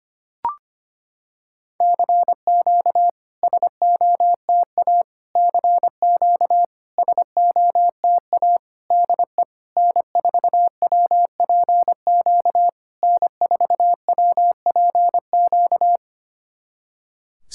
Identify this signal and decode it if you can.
Morse code, heard as a keyed tone, about 25 wpm, 700 Hz. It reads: CQ SOTA CQ SOTA DE N4WPQ N4WPQ
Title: SOTA QSO Practice 25wpm